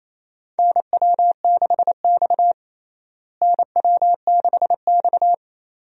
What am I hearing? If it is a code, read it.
Morse code, heard as a keyed tone, about 28 wpm, 700 Hz. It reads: NW6X NW6X
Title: USA Call Signs - 4 Characters - Repeated 2x - 28wpm